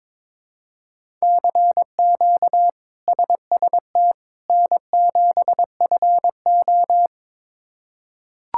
Morse code, heard as a keyed tone, about 22 wpm, 700 Hz. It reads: CQ SST N7FO
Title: K1USN SST QSO Practice - 22wpm